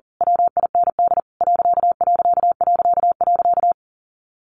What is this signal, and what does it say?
Morse code, heard as a keyed tone, about 40 wpm, 700 Hz. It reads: WIND ....
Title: A Sound Of Thunder 40wpm